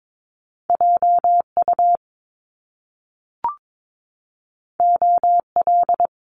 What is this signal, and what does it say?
Morse code, heard as a keyed tone, about 22 wpm, 700 Hz. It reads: JU OL
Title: Top Two Letter Combinations in Words 22wpm